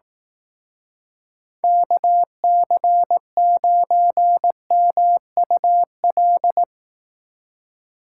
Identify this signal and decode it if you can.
Morse code, heard as a keyed tone, about 18 wpm, 700 Hz. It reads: KC9MUL
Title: Speed Racing - USA 2x3 Call Signs 15wpm